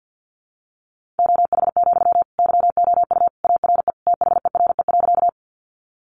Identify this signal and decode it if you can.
Morse code, heard as a keyed tone, about 50 wpm, 700 Hz. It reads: OH, YOU ARE THERE.
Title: Sentences from Top 500 Words 50wpm